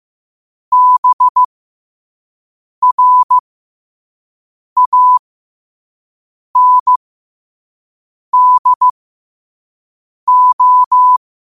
Morse code, heard as a keyed tone, about 15 wpm, spaced as for 5 wpm, 1000 Hz. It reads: BRANDO
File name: qotdfiles / 5wpm